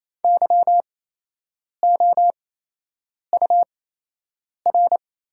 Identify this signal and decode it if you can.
Morse code, heard as a keyed tone, about 28 wpm, spaced as for 8 wpm, 700 Hz. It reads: YOUR
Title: 8x Character Spacing - Top 100 Words - 28wpm